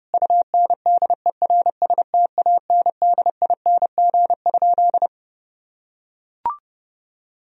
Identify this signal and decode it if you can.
Morse code, heard as a keyed tone, about 32 wpm, 700 Hz. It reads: UNDERSTANDING?
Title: Sentences from Top 1000 Words - Review Part 3 - 30wpm